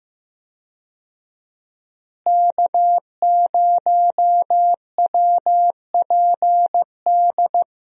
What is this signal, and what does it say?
Morse code, heard as a keyed tone, about 15 wpm, 700 Hz. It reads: K0WPD
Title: USA Call Signs - 5 Characters - Repeated 2x - 15wpm